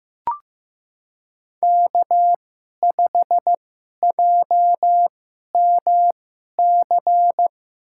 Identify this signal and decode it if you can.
Morse code, heard as a keyed tone, about 15 wpm, spaced as for 11 wpm, 700 Hz. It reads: K5JMC
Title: USA Call Signs - 5 Characters - 2x Character Spacing - 15wpm